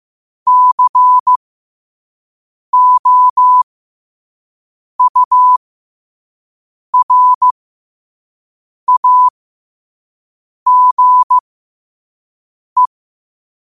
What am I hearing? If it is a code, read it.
Morse code, heard as a keyed tone, about 15 wpm, spaced as for 5 wpm, 1000 Hz. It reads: COURAGE
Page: Quotes for Tue, 12 Aug 2025 in Morse Code at 5 words per minute.